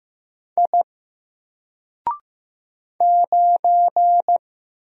Morse code, heard as a keyed tone, about 15 wpm, 700 Hz. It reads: I 9
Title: Single Letter-Number 15wpm